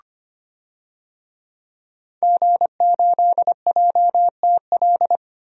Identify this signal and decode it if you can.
Morse code, heard as a keyed tone, about 25 wpm, 700 Hz. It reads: G8JTL
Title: DXCC-100 Call Signs - 5 Characters - 25wpm